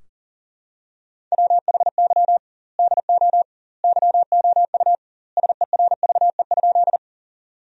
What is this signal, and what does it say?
Morse code, heard as a keyed tone, about 40 wpm, 700 Hz. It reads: WHY DO YOU SERVE?
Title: 2x Word-Spacing - Sentences from Top 400 Words - 40wpm